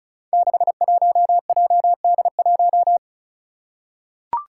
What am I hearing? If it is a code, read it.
Morse code, heard as a keyed tone, about 36 wpm, 700 Hz. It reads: 61JD1
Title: Five Challenging Letter-Number Combinations 35wpm